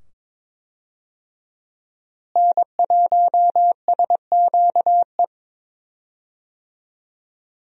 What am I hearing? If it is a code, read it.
Morse code, heard as a keyed tone, about 22 wpm, 700 Hz. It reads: N1SQE
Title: USA Call Signs - 5 Characters - Repeated 2x - 22wpm